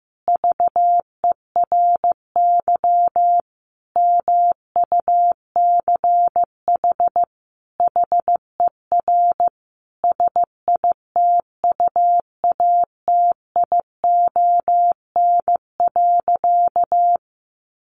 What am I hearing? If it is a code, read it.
Morse code, heard as a keyed tone, about 14 wpm, 700 Hz. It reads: VERY MUCH HER SITUATION.
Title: Sentences from Top 1200 Words 15wpm